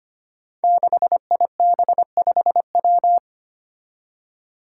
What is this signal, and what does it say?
Morse code, heard as a keyed tone, about 25 wpm, 700 Hz. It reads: 6IB5W
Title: Five Challenging Letter-Number Combinations 25wpm